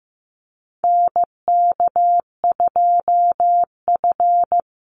Morse code, heard as a keyed tone, about 15 wpm, 700 Hz. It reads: NK2F
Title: Call Signs Repeated 3x - 15wpm